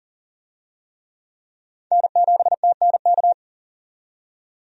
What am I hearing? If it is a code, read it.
Morse code, heard as a keyed tone, about 40 wpm, 700 Hz. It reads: N7TNK
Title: USA Call Signs - 5 Characters - 40wpm